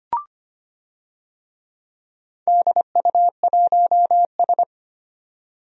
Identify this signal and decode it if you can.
Morse code, heard as a keyed tone, about 25 wpm, 700 Hz. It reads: DU1S
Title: DXCC-100 Call Signs - 4 Characters - 25wpm